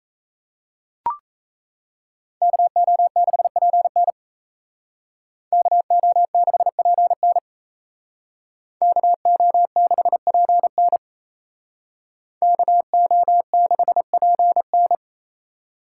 Morse code, heard as a keyed tone, about 33 wpm, 700 Hz. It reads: KO6PN KO6PN KO6PN KO6PN
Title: Speed Racing - USA 2x2 Call Signs 28wpm